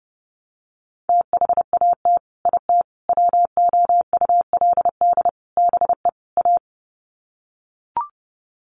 Morse code, heard as a keyed tone, about 28 wpm, 700 Hz. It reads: THAT IT WOULD BE A
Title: Sets of 5 Words Encore2 30wpm